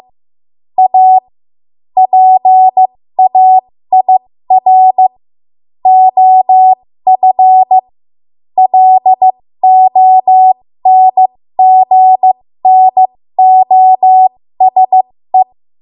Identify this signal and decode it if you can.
Morse code, heard as a keyed tone, about 15 wpm, spaced as for 13 wpm, 750 Hz. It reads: A PAIR OF LONGNOSE
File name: Morse / 13WPM